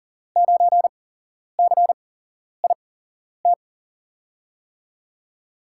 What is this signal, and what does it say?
Morse code, heard as a keyed tone, about 37 wpm, spaced as for 11 wpm, 700 Hz. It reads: 9CIT
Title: USA Call Signs - 5 Characters - 8x Character Spacing - 40wpm